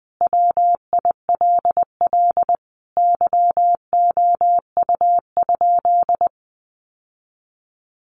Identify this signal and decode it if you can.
Morse code, heard as a keyed tone, about 20 wpm, 700 Hz. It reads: WILL YOU?
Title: Sentences from Top 1400 Words 20wpm